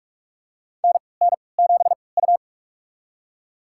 Morse code, heard as a keyed tone, about 43 wpm, spaced as for 25 wpm, 700 Hz. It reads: NN7U
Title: USA Call Signs - 4 Characters - 3x Character Spacing - 45wpm